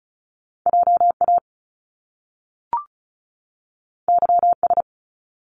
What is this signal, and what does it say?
Morse code, heard as a keyed tone, about 36 wpm, 700 Hz. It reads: JA YS
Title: Top Two Letter Combinations in Words 35wpm